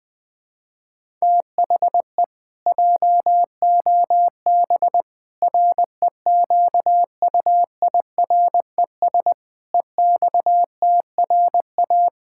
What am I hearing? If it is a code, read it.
Morse code, heard as a keyed tone, about 20 wpm, 700 Hz. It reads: THE JOB REQUIRES EXTRA
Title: Pangram Challenge v2 - 20wpm